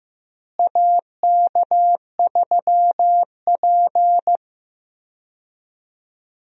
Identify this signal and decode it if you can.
Morse code, heard as a keyed tone, about 15 wpm, 700 Hz. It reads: AK3P
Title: USA Call Signs - 4 Characters - Repeated 2x - 15wpm